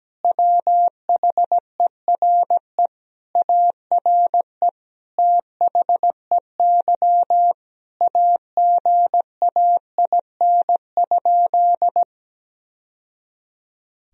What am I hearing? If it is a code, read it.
Morse code, heard as a keyed tone, about 17 wpm, 700 Hz. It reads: WHERE ARE THEY AGAIN?
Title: Sentences from Top 1000 Words - Review Part 3 - 17wpm